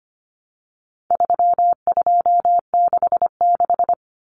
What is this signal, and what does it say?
Morse code, heard as a keyed tone, about 25 wpm, 700 Hz. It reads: 3266
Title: Numbers - 4-Digit 25wpm